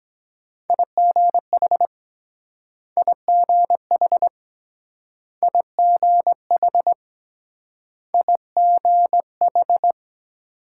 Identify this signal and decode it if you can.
Morse code, heard as a keyed tone, about 23 wpm, 700 Hz. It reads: IGH IGH IGH IGH